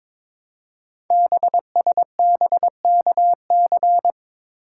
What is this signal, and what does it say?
Morse code, heard as a keyed tone, about 21 wpm, 700 Hz. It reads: BSBKC
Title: Code Groups - 5 Letters - 22wpm